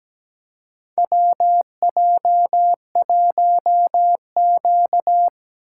Morse code, heard as a keyed tone, about 17 wpm, 700 Hz. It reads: WJ1Q
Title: USA Call Signs - 2x Character Spacing - 17wpm